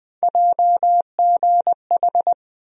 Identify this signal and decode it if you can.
Morse code, heard as a keyed tone, about 20 wpm, 700 Hz. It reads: JGH
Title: USA Call Sign Postfixes - 3 Letters 20wpm